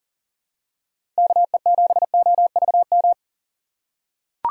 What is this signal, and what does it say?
Morse code, heard as a keyed tone, about 40 wpm, 700 Hz. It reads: KE7OVM
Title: USA Call Signs - 6 Characters - 3x Character Spacing - 40wpm